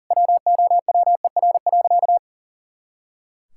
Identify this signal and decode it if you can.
Morse code, heard as a keyed tone, about 40 wpm, 700 Hz. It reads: WOWER.
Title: Pangram Challenge v2 - 40wpm